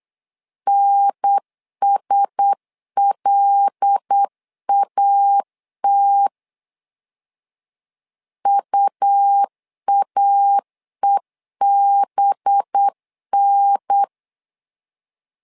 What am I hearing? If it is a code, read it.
Morse code, heard as a keyed tone, about 8 wpm, 800 Hz. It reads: NSLAT UAEBN
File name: LEZIONI / 5 LBVC / 5 LB 8wpm 2816